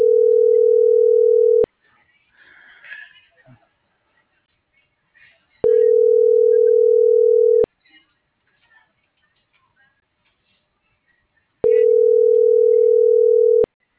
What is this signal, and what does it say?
Ringback tone (North America), 440 + 480 Hz.